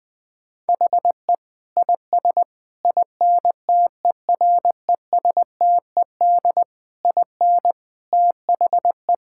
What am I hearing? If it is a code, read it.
Morse code, heard as a keyed tone, about 20 wpm, 700 Hz. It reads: HE IS INTERESTED IN THE
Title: Sentences from Top 1000 Words - Review Part 6 - 20wpm